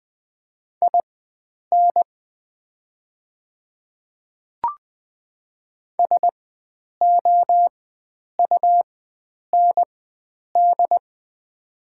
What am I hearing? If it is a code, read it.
Morse code, heard as a keyed tone, about 20 wpm, spaced as for 9 wpm, 700 Hz. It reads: IN SOUND
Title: 4x Character Spacing - Top 300 Words - 20wpm